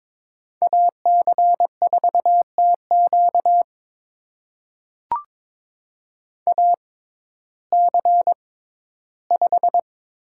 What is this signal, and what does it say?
Morse code, heard as a keyed tone, about 22 wpm, spaced as for 13 wpm, 700 Hz. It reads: AC4TQ AC5
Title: USA Call Signs - 5 Characters - 6x Character Spacing - 22wpm